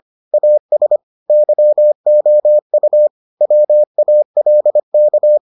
Morse code, heard as a keyed tone, about 25 wpm, 600 Hz. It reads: AS YOU WALK